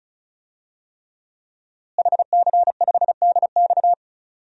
Morse code, heard as a keyed tone, about 34 wpm, 700 Hz. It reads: HC5DX
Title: DXCC-100 Call Signs - 5 Characters - Repeated 3x - 35wpm